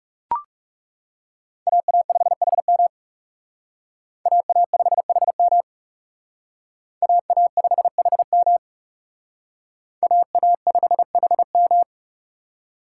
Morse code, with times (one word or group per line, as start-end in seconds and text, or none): 1.7-2.9 AA5HM
4.3-5.6 AA5HM
7.0-8.6 AA5HM
10.0-11.8 AA5HM